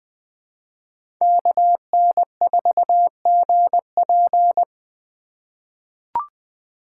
Morse code, heard as a keyed tone, about 20 wpm, 700 Hz. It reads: KN4GP
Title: USA Call Signs - 5 Characters - 4x Character Spacing - 20wpm